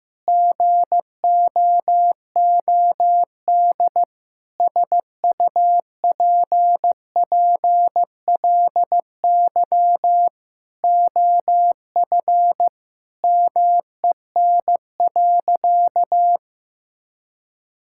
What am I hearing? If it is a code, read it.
Morse code, heard as a keyed tone, about 16 wpm, 700 Hz. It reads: GOOD SUPPLY OF MEN.
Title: Sentences from Top 1000 Words - Review Part 5 - 15wpm